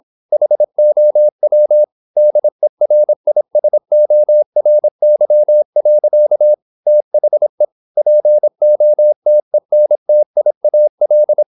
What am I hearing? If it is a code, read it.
Morse code, heard as a keyed tone, about 26 wpm, 600 Hz. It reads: HOW DERISORY. THE POTENTIAL